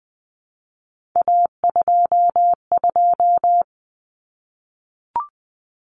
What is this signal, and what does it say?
Morse code, heard as a keyed tone, about 20 wpm, 700 Hz. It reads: A22
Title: Common Cut Numbers - 3-Digit 20wpm